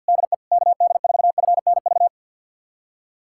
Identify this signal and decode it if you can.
Morse code, heard as a keyed tone, about 53 wpm, 700 Hz. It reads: DE KD4FNV
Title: USA Callsign DE Callsign 50wpm